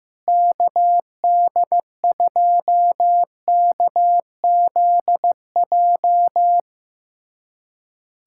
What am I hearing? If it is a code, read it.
Morse code, heard as a keyed tone, about 15 wpm, 700 Hz. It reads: KD2KZJ
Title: USA Callsign DE Callsign 15wpm